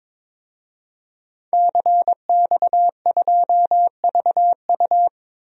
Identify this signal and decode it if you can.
Morse code, heard as a keyed tone, about 21 wpm, 700 Hz. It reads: CX2VU
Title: DXCC-100 Call Signs - 5 Characters - 22wpm